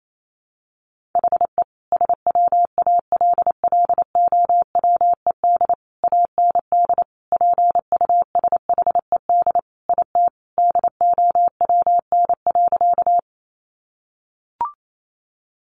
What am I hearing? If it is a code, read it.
Morse code, heard as a keyed tone, about 28 wpm, 700 Hz. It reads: HE SWALLOWED AND PUSHED IT DOWN.
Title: A Sound Of Thunder 28wpm